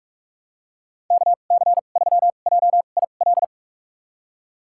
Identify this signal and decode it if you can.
Morse code, heard as a keyed tone, about 46 wpm, spaced as for 33 wpm, 700 Hz. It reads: KC3JIL
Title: USA Call Signs - 6 Characters - 2x Character Spacing - 45wpm